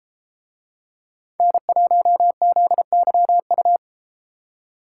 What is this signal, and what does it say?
Morse code, heard as a keyed tone, about 32 wpm, 700 Hz. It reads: N1ZYU